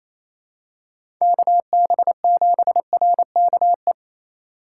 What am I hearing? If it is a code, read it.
Morse code, heard as a keyed tone, about 28 wpm, 700 Hz. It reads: KB7RKE
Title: USA Call Signs - 2x Character Spacing - 28wpm